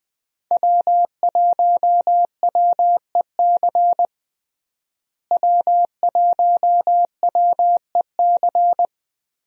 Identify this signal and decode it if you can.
Morse code, heard as a keyed tone, about 20 wpm, 700 Hz. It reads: W1WEC W1WEC
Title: USA Call Signs - 5 Characters - Repeated 3x - 20wpm